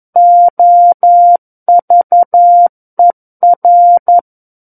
Morse code, heard as a keyed tone, about 11 wpm, 700 Hz. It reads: OVER
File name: 11 WPM